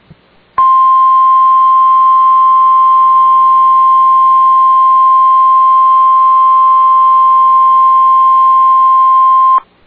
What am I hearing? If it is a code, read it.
Emergency Alert System (SAME) tones (the 1050 Hz attention tone). It is heard from 0.5 to 9.6 s.